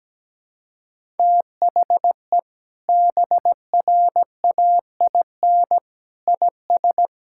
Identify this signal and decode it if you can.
Morse code, heard as a keyed tone, about 17 wpm, 700 Hz. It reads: THE BRAIN IS
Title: Sentences from Top 1000 Words - Review Part 6 - 17wpm